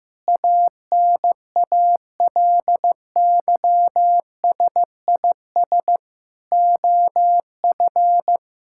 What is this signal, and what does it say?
Morse code, heard as a keyed tone, about 15 wpm, 700 Hz. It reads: ANALYSIS OF
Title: Sentences from Top 1000 Words - Review Part 6 - 15wpm